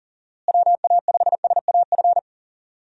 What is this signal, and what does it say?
Morse code, heard as a keyed tone, about 40 wpm, 700 Hz. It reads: WA5SAF
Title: USA Call Signs - 6 Characters - Repeated 2x - 40wpm